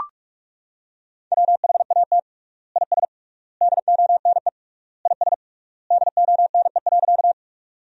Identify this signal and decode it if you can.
Morse code, heard as a keyed tone, about 43 wpm, 700 Hz. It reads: WHAT IS DONE IS DONE.
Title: 3x Word-Spacing - Sentences from Top 500 Words - 45wpm